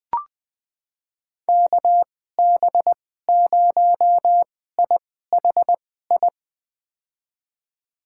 Morse code, heard as a keyed tone, about 21 wpm, spaced as for 15 wpm, 700 Hz. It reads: KB0IHI
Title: USA Call Signs - 2x Character Spacing - 20wpm